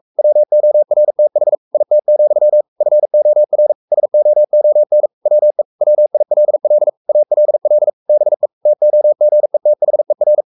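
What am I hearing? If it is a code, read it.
Morse code, heard as a keyed tone, about 44 wpm, 600 Hz. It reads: WORTH IT, FOR SOON WE WILL ALL BE TOGETHER
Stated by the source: Morse code at 43 wpm sending Ken Follett's novel Never in English, from CW med Gnister.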